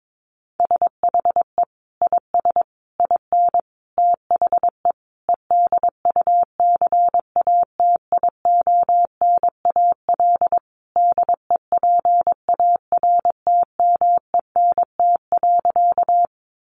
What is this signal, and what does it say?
Morse code, heard as a keyed tone, about 22 wpm, 700 Hz. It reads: SHE IS IN THE EDUCATIONAL DEPARTMENT.
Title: Sentences from Top 1200 Words 22wpm